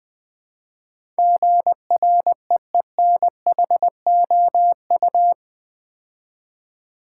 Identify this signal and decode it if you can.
Morse code, heard as a keyed tone, about 20 wpm, 700 Hz. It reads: GREENHOU
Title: Build A Word - 4 Letters At A Time - 10-Letter Words - 20wpm